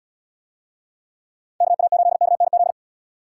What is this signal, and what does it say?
Morse code, heard as a keyed tone, about 69 wpm, 700 Hz. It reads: KA8KNZ